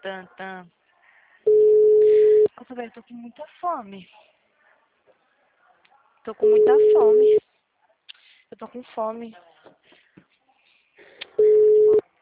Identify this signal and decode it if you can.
Ringback tone (Europe, 425 Hz), with speech.